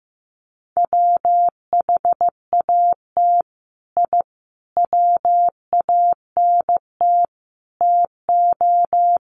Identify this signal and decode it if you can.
Morse code, heard as a keyed tone, about 15 wpm, 700 Hz. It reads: WHAT I WANT TO
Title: Sets of 5 Words Encore 15wpm